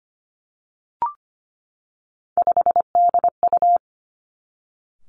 Morse code, heard as a keyed tone, about 25 wpm, 700 Hz. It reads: 5DU
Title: Three Challenging Letter-Number Combinations 25wpm